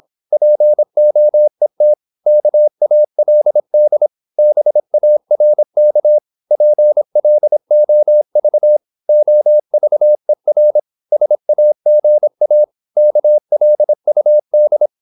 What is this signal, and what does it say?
Morse code, heard as a keyed tone, about 26 wpm, 600 Hz. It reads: POET KALD BARK PLOV OVER SAGA KLUD
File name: Korte ord DK 26wpm